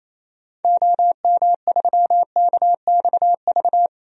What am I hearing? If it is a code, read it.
Morse code, heard as a keyed tone, about 28 wpm, 700 Hz. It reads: OM3KXV
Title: DXCC-100 Call Signs - 6 Characters - 28wpm